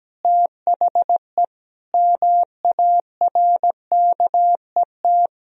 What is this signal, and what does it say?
Morse code, heard as a keyed tone, about 17 wpm, 700 Hz. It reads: THE MARKET
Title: Sentences from Top 1000 Words - Review Part 3 - 17wpm